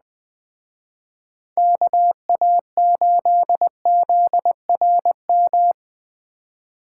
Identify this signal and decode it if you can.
Morse code, heard as a keyed tone, about 20 wpm, 700 Hz. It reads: KA8ZRM